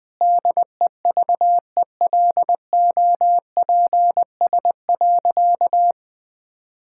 Morse code, heard as a keyed tone, about 21 wpm, 700 Hz. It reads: DEVELOPS.
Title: Sentences with 7-Word Phrases - Spoken Beginning - 20wpm